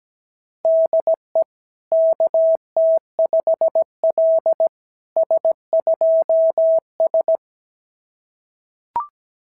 Morse code, heard as a keyed tone, about 17 wpm, 660 Hz. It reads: DE KT5L S2S
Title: SOTA QSO Practice 17wpm